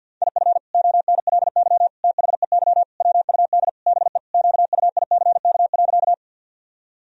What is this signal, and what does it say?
Morse code, heard as a keyed tone, about 51 wpm, 700 Hz. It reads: IF ONLY THEY WUD BE QUICK.
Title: Sentences from Top 500 Words and Abbreviations 50wpm